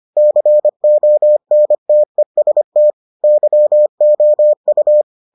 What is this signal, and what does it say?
Morse code, heard as a keyed tone, about 25 wpm, 600 Hz. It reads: CONTEST YOU